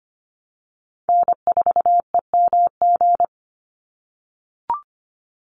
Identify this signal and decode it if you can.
Morse code, heard as a keyed tone, about 25 wpm, 700 Hz. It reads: N4EMG
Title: NA SOTA Chaser Callsigns 25wpm